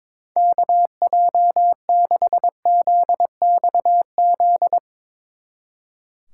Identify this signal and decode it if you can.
Morse code, heard as a keyed tone, about 22 wpm, 700 Hz. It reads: KJ6ZXZ
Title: Speed Racing - USA 2x3 Call Signs 22wpm